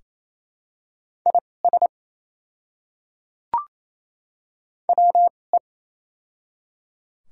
Morse code, heard as a keyed tone, about 28 wpm, spaced as for 20 wpm, 700 Hz. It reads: IS WE
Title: 2x Character Spacing - Top 100 Words - 28wpm